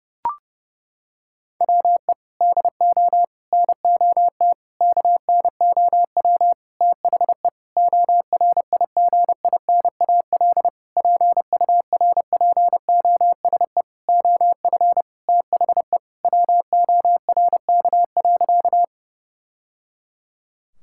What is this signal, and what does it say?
Morse code, heard as a keyed tone, about 30 wpm, 700 Hz. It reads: WE DO NOT KNOW THE ORIGINAL PURPOSE OF THE WORK.
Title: Sentences from Top 1000 Words - Review Part 6 - 30wpm